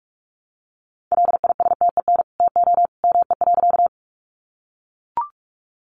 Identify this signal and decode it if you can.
Morse code, heard as a keyed tone, about 44 wpm, 700 Hz. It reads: LISTEN TO ME.
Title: Sentences from Top 400 Words 45wpm